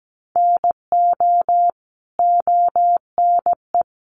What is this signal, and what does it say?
Morse code, heard as a keyed tone, about 17 wpm, 700 Hz. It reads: NO ONE
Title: Sentences from Top 1100 Words 17wpm